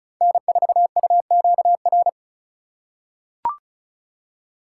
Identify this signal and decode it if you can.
Morse code, heard as a keyed tone, about 34 wpm, 700 Hz. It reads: N4UQR
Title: USA Call Signs - 3x Character Spacing - 35wpm